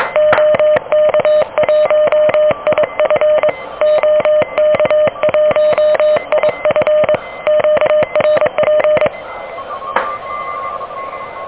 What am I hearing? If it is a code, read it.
Morse code, heard as a keyed tone, about 22 wpm, 600 Hz. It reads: OK1IF OK1IF QRP
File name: OK QRP 2023 / OK QRP 2023